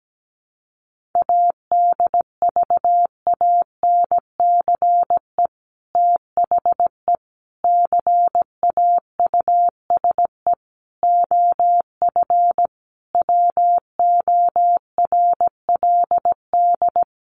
Morse code, heard as a keyed tone, about 17 wpm, 700 Hz. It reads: ADVANCE THE CAUSE OF WORLD
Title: Sentences from Top 1400 Words 17wpm